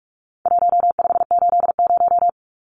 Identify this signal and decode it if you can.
Morse code, heard as a keyed tone, about 44 wpm, 700 Hz. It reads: 1580
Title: Numbers - Contest Style - 4-Digit 45wpm